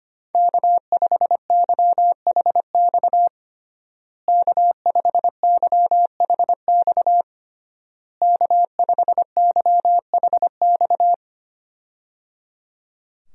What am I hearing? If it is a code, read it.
Morse code, heard as a keyed tone, about 25 wpm, 700 Hz. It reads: K5YHX K5YHX K5YHX
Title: USA Call Signs - 5 Characters - Repeated 3x - 25wpm